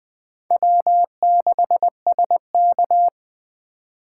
Morse code, heard as a keyed tone, about 20 wpm, 700 Hz. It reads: W6SK
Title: Speed Racing - USA 1x2 Call Signs 17wpm